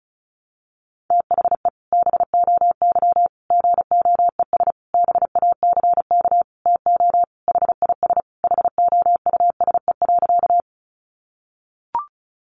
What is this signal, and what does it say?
Morse code, heard as a keyed tone, about 34 wpm, 700 Hz. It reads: THE BOY GOES BACK TO HIS HOUSE.
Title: Easy Phrases 35wpm